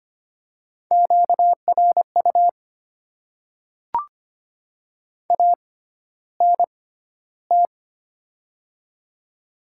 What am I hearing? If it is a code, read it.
Morse code, heard as a keyed tone, about 25 wpm, spaced as for 13 wpm, 690 Hz. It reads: QRU ANT
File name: Top 90 QSO Elements - 6x Character Spacing - 25wpm